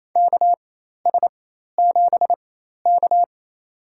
Morse code, heard as a keyed tone, about 27 wpm, spaced as for 13 wpm, 700 Hz. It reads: KS7K
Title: USA Call Signs - 4 Characters - 4x Character Spacing - 28wpm